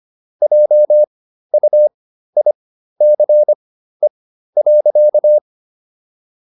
Morse code, heard as a keyed tone, about 25 wpm, spaced as for 13 wpm, 600 Hz. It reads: JUICE.